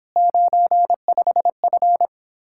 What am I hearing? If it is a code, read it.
Morse code, heard as a keyed tone, about 31 wpm, 700 Hz. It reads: 95F